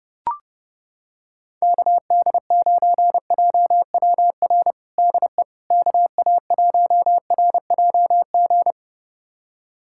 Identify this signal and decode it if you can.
Morse code, heard as a keyed tone, about 30 wpm, 700 Hz. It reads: KD9JWR DE KA1RJG
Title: USA Callsign DE Callsign 30wpm